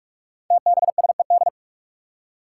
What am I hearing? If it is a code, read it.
Morse code, heard as a keyed tone, about 44 wpm, 700 Hz. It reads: TBSED
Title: Five Challenging Letter-Number Combinations 45wpm